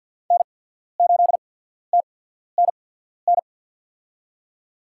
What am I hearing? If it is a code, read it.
Morse code, heard as a keyed tone, about 53 wpm, spaced as for 14 wpm, 700 Hz. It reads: N8TNN
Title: USA Call Signs - 5 Characters - 8x Character Spacing - 50wpm